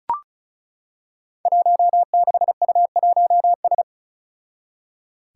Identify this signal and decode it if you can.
Morse code, heard as a keyed tone, about 34 wpm, 700 Hz. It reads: 16U1S
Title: Five Challenging Letter-Number Combinations 35wpm